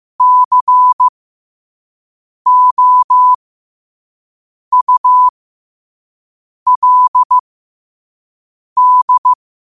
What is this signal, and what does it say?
Morse code, heard as a keyed tone, about 15 wpm, spaced as for 5 wpm, 1000 Hz. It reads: COULD
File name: qotdfiles / 5wpm